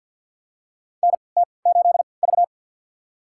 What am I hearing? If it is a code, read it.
Morse code, heard as a keyed tone, about 53 wpm, spaced as for 29 wpm, 700 Hz. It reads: NT8V
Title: USA Call Signs - 3x Character Spacing - 50wpm